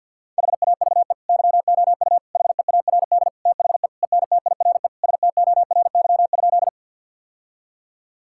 Morse code, heard as a keyed tone, about 50 wpm, 700 Hz. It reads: HAVE YOU HEARD THE ENTIRE STORY?
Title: Sentences from Top 1000 Words - Review Part 5 - 50wpm